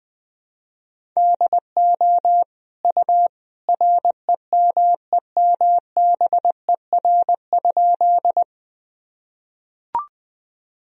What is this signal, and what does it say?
Morse code, heard as a keyed tone, about 20 wpm, 700 Hz. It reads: DO U REMEMBER?
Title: Sentences from Top 400 Words and Abbreviations 20wpm